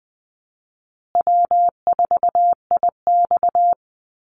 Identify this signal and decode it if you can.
Morse code, heard as a keyed tone, about 20 wpm, 700 Hz. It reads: W4IX
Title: Easy Call Signs Repeated 3x - 20wpm